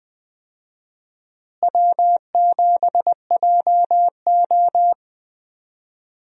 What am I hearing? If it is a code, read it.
Morse code, heard as a keyed tone, about 20 wpm, 700 Hz. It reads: W7JO
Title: USA Call Signs - 4 Characters - Repeated 2x - 20wpm